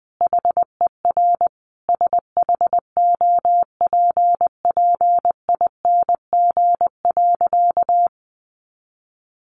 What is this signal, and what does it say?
Morse code, heard as a keyed tone, about 20 wpm, 700 Hz. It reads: HER SHOPPING.
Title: Additional Easy Phrases 20wpm